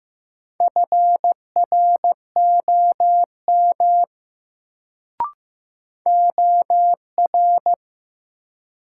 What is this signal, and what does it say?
Morse code, heard as a keyed tone, about 15 wpm, 700 Hz. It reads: FROM OR
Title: Sentences from Top 100 Words - New Word Drill - 15wpm